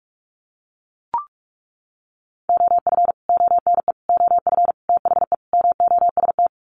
Morse code, heard as a keyed tone, about 44 wpm, 700 Hz. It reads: OF ONE OF THE MOST
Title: Sets of 5 Words Encore2 45wpm